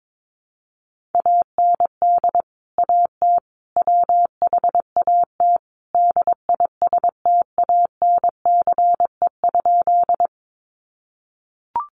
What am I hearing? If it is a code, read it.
Morse code, heard as a keyed tone, about 22 wpm, 700 Hz. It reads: AND AT WHAT DISTANCE?
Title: Sentences from Top 1200 Words 22wpm